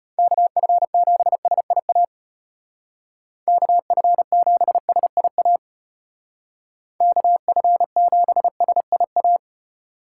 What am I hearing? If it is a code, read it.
Morse code, heard as a keyed tone, about 30 wpm, 700 Hz. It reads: KF7SIA KF7SIA KF7SIA
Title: Speed Racing - USA 2x3 Call Signs 25wpm